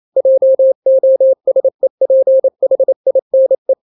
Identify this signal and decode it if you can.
Morse code, heard as a keyed tone, about 28 wpm, 530 Hz. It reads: JOSEPHINE
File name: daily / 27wpm